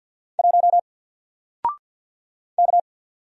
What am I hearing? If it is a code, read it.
Morse code, heard as a keyed tone, about 50 wpm, 700 Hz. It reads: 1 K
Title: Single Character (Letter:Number:Punctuation:Prosign) 50wpm